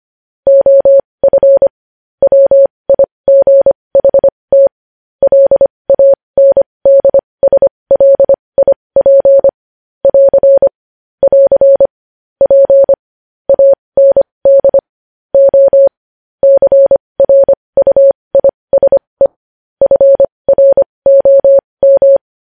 Morse code, heard as a keyed tone, about 25 wpm, spaced as for 20 wpm, 550 Hz. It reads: OF WIGHT LANDSLIP <AR> <AR> P AND O CRUISE FROM
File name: podcast / News headlines-1x-25x20